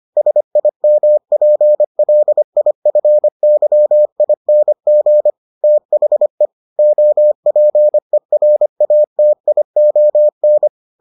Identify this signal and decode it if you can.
Morse code, heard as a keyed tone, about 25 wpm, 600 Hz. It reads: SIMPLIFYING THE OPERATION